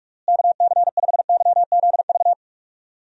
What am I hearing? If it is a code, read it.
Morse code, heard as a keyed tone, about 46 wpm, 700 Hz. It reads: KC5YZV
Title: USA Call Signs - 6 Characters - Repeated 3x - 45wpm